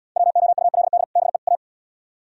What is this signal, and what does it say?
Morse code, heard as a keyed tone, about 76 wpm, 700 Hz. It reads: WOULD BE A